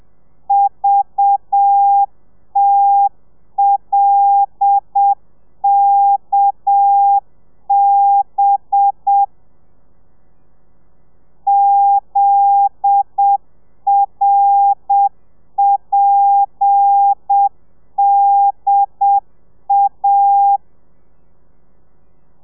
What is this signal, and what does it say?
Morse code, heard as a keyed tone, about 7 wpm, 800 Hz. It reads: VTLKB ZRPDA
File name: telegrafia / i2tfj / Lez10 30